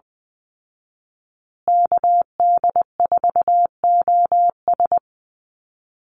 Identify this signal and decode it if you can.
Morse code, heard as a keyed tone, about 20 wpm, 700 Hz. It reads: KD4OS
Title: Call Signs Repeated 3x - 20wpm